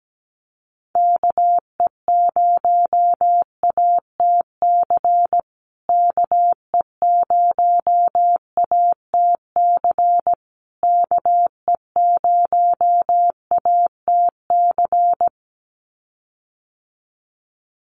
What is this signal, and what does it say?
Morse code, heard as a keyed tone, about 17 wpm, 700 Hz. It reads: KE0ATC KE0ATC KE0ATC
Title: USA Call Signs Repeated 3x 17wpm